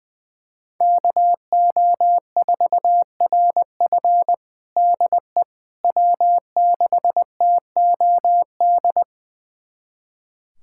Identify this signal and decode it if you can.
Morse code, heard as a keyed tone, about 20 wpm, 700 Hz. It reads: KO4RF DE W6TOD
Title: USA Callsign DE Callsign - 5 Characters - 20wpm